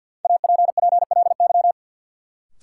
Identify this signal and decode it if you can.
Morse code, heard as a keyed tone, about 46 wpm, 700 Hz. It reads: APPLY
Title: Sentences from Top 1200 Words - New Word Drill - 50wpm